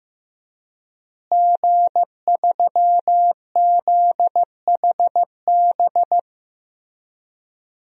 Morse code, heard as a keyed tone, about 15 wpm, 700 Hz. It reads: G3ZHB
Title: DXCC-100 Call Signs - 5 Characters - Repeated 3x - 15wpm